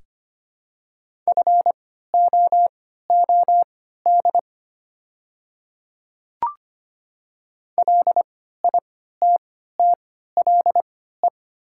Morse code, heard as a keyed tone, about 25 wpm, spaced as for 14 wpm, 700 Hz. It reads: FOOD LITTLE
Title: 3x Character Spacing - Top 300 Words - 25wpm